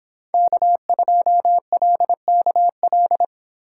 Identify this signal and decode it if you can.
Morse code, heard as a keyed tone, about 26 wpm, 700 Hz. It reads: K2LKL
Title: Speed Racing - USA 1x3 Call Signs 22wpm